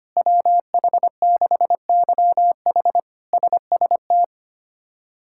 Morse code, heard as a keyed tone, about 25 wpm, 700 Hz. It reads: WH6YH SST
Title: K1USN SST QSO Practice - 25wpm